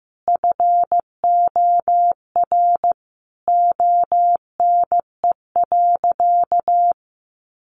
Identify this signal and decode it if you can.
Morse code, heard as a keyed tone, about 15 wpm, 700 Hz. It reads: FOR ONE.
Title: Sentences from Top 300 Words 15wpm